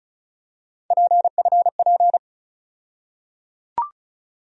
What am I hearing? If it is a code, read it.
Morse code, heard as a keyed tone, about 36 wpm, 700 Hz. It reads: PFP
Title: USA Call Sign Postfixes - 3 Letters 35wpm